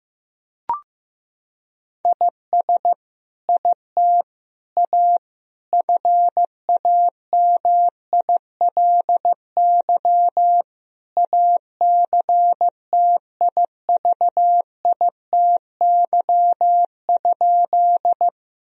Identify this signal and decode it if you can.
Morse code, heard as a keyed tone, about 16 wpm, 700 Hz. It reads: IS IT A FAMILY ACTIVITY?
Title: Sentences from Top 1000 Words - Review Part 4 - 15wpm